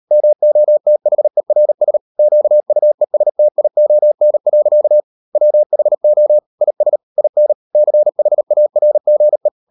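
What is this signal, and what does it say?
Morse code, heard as a keyed tone, about 38 wpm, 600 Hz. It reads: MOTHERS QUESTION. WHO IS IN CHARGE
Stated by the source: Morse code practice at 38 WPM, sending Ken Follett's novel Never in English, from CW med Gnister.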